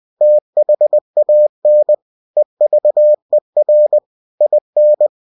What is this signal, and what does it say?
Morse code, heard as a keyed tone, about 20 wpm, 600 Hz. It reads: THAN EVER IN
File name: cw-books / winnie-the-pooh / winnie-20wpm-page-0023